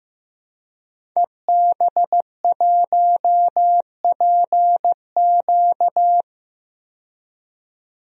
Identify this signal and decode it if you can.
Morse code, heard as a keyed tone, about 15 wpm, 700 Hz. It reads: EB1PQ
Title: DXCC-100 Call Signs - 5 Characters - 15wpm